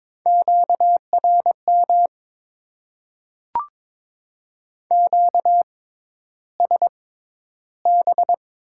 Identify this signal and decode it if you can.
Morse code, heard as a keyed tone, about 22 wpm, spaced as for 11 wpm, 700 Hz. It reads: QRM QSB
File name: Top 90 QSO Elements - 6x Character Spacing - 22wpm